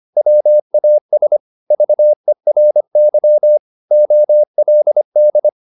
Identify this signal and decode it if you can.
Morse code, heard as a keyed tone, about 25 wpm, 600 Hz. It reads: WAS VERY OLD
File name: the old man and the sea part 1 25wpm